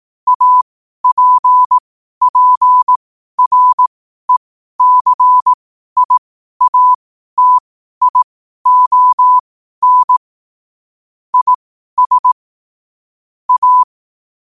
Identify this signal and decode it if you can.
Morse code, heard as a keyed tone, about 18 wpm, spaced as for 12 wpm, 1000 Hz. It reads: APPRECIATION IS A
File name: qotdfiles / 12wpm